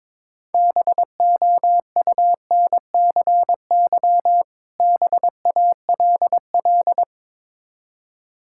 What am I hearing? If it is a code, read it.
Morse code, heard as a keyed tone, about 22 wpm, 700 Hz. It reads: BOUNCY BALL
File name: Difficult Words with Hints - 22wpm